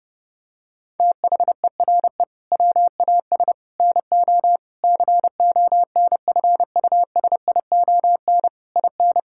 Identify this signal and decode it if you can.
Morse code, heard as a keyed tone, about 32 wpm, 700 Hz. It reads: THERE WAS NO CONFUSION IN
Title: Sentences from Top 1600 Words 30wpm